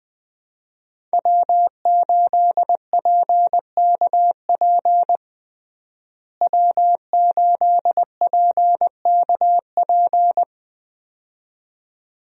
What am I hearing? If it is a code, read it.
Morse code, heard as a keyed tone, about 21 wpm, 700 Hz. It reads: W8PKP W8PKP
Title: USA Call Signs - 5 Characters - Repeated 2x - 20wpm